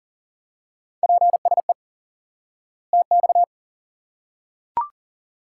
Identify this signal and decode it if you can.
Morse code, heard as a keyed tone, about 39 wpm, 700 Hz. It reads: PSE TX
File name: IQE Course 3 - Lesson 89 - YAGI - Two Elements - 40wpm